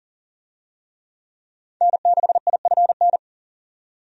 Morse code, heard as a keyed tone, about 40 wpm, 700 Hz. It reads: N6IFN
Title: USA Call Signs - 5 Characters - 40wpm